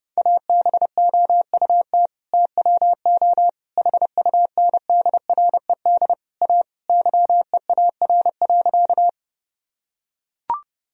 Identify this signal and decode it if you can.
Morse code, heard as a keyed tone, about 30 wpm, 700 Hz. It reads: ABOUT TWO HUNDRED A YEAR.
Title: Sentences from Top 1000 Words - Review Part 2 - 30wpm